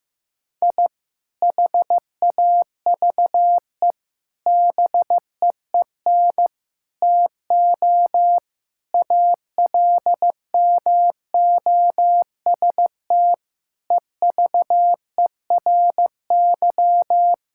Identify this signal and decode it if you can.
Morse code, heard as a keyed tone, about 16 wpm, 700 Hz. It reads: I HAVE BEEN TO ALMOST EVERY
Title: Sentences from Top 1000 Words - Review Part 5 - 15wpm